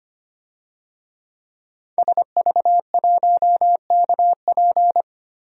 Morse code, heard as a keyed tone, about 25 wpm, 700 Hz. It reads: SV1KP
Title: DXCC-100 Call Signs - 5 Characters - Repeated 2x - 25wpm